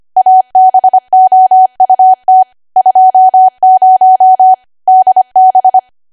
Morse code, heard as a keyed tone, about 25 wpm, 750 Hz. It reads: ABOUT 20 DB
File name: Archive / 25 WPM / 25WPM